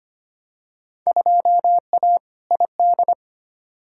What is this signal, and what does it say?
Morse code, heard as a keyed tone, about 25 wpm, 700 Hz. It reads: 2A ID
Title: Field Day Exchanges Encore - 25wpm